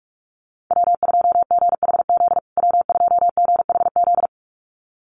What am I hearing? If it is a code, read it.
Morse code, heard as a keyed tone, about 44 wpm, 700 Hz. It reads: W2GHZ W2GHZ
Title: USA Call Signs Repeated 2x 45wpm